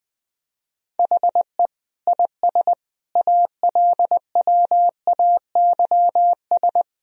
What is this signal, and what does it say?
Morse code, heard as a keyed tone, about 21 wpm, 700 Hz. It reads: HE IS ALWAYS
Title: Sentences from Top 1000 Words - Review Part 3 - 20wpm